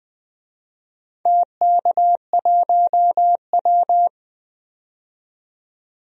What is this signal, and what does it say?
Morse code, heard as a keyed tone, about 21 wpm, 700 Hz. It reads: TK1W
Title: DXCC-100 Call Signs - 4 Characters - 20wpm